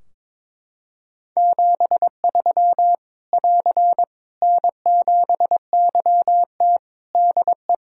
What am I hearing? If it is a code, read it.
Morse code, heard as a keyed tone, about 22 wpm, 700 Hz. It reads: 73 <AR> N7YT DE
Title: Practice QSO - Exchange 4 - Ending - 22wpm